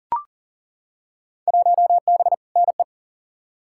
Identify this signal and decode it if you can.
Morse code, heard as a keyed tone, about 37 wpm, 700 Hz. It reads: 1B NE
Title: Field Day Exchanges Encore - 40wpm